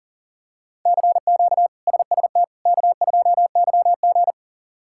Morse code, heard as a keyed tone, about 40 wpm, 700 Hz. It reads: CQ SST K2YG
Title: K1USN SST QSO Practice - 40wpm